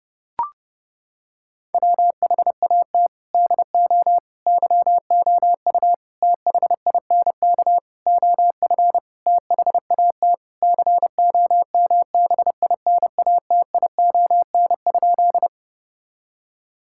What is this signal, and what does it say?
Morse code, heard as a keyed tone, about 30 wpm, 700 Hz. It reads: WHAT DO YOU THINK OF THAT COMBINATION?
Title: Sentences from Top 1000 Words - Review Part 6 - 30wpm